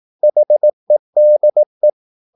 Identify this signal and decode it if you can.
Morse code, heard as a keyed tone, about 18 wpm, 600 Hz. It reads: HEDE
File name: Korte ord DK 18wpm